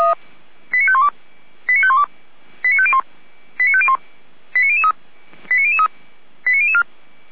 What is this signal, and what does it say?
Five-tone selective call (ZVEI), heard as quick runs of five tones, tones of about 70 ms. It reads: 89512 89512 89501 89501 89003 89003 89004
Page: audio.f85360.feuerwehr-orlamuende.org